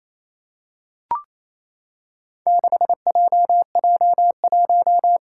Morse code, heard as a keyed tone, about 28 wpm, 700 Hz. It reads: 6JJ1
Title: Four Challenging Letter-Number Combinations 28wpm